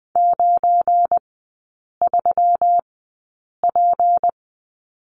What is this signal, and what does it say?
Morse code, heard as a keyed tone, about 20 wpm, spaced as for 8 wpm, 700 Hz. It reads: 93P
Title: 172 Lesson 28e - Number 9 - Three Characters 20wpm